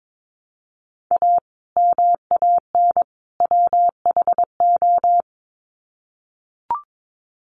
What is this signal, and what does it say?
Morse code, heard as a keyed tone, about 22 wpm, 700 Hz. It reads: A MAN WHO
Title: Sets of 3 Words 22wpm